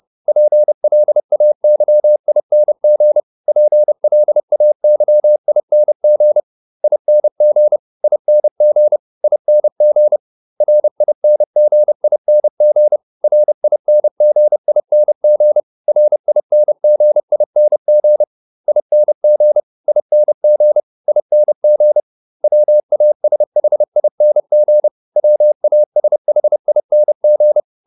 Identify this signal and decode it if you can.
Morse code, heard as a keyed tone, about 30 wpm, 600 Hz. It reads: PLAYING PLAYING ING ING ING RINGING RINGING RINGING ING ING ING WASHING WASHING